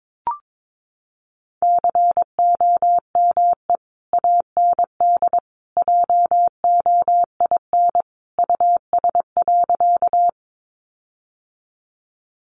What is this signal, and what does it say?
Morse code, heard as a keyed tone, about 22 wpm, 700 Hz. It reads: COME AND JOIN US.
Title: Sentences from Top 1400 Words 22wpm